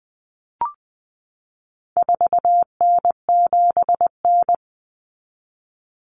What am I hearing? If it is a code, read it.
Morse code, heard as a keyed tone, about 20 wpm, 700 Hz. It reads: 4N7N
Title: Common Cut Numbers - 4-Digit 20wpm